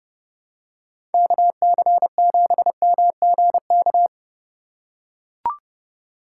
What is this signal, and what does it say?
Morse code, heard as a keyed tone, about 29 wpm, 700 Hz. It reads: KC7MGK